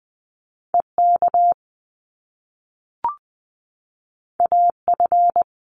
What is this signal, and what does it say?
Morse code, heard as a keyed tone, about 20 wpm, 700 Hz. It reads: EK AF
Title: Top Two Letter Combinations in Words 20wpm